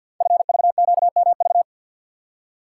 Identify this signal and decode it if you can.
Morse code, heard as a keyed tone, about 48 wpm, 700 Hz. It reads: FVQMV
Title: Code Groups - 5 Letter:Numbers - 50wpm